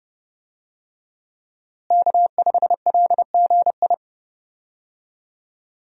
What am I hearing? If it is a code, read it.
Morse code, heard as a keyed tone, about 30 wpm, 700 Hz. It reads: K5LGI
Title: USA Call Signs - 5 Characters - Repeated 3x - 30wpm